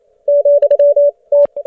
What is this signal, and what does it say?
Morse code, heard as a keyed tone, about 28 wpm, spaced as for 22 wpm, 580 Hz. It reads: ,T